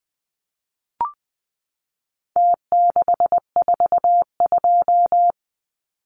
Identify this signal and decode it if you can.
Morse code, heard as a keyed tone, about 20 wpm, 700 Hz. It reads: T642
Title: Numbers - Contest Style - 4-Digit 20wpm